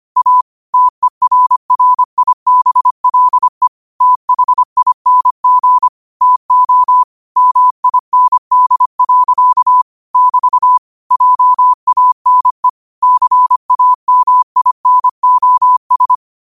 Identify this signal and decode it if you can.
Morse code, heard as a keyed tone, about 25 wpm, 1000 Hz. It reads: A TERRIBLE THING TO MIND. <BT> JANE CAMINOS
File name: qotdfiles / 25wpm